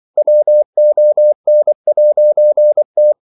Morse code, heard as a keyed tone, about 24 wpm, 600 Hz. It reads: WON'T
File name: CW / Practice / EnglishQuotations0224